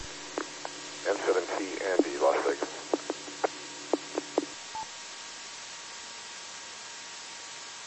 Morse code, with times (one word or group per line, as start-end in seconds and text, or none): none